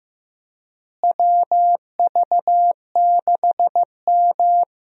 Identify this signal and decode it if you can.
Morse code, heard as a keyed tone, about 15 wpm, 700 Hz. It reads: WV6M
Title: USA Call Signs - 4 Characters - Repeated 3x - 15wpm